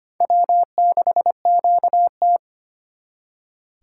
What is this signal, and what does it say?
Morse code, heard as a keyed tone, about 31 wpm, 700 Hz. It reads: W6QT
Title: Speed Racing - USA Call Signs 25wpm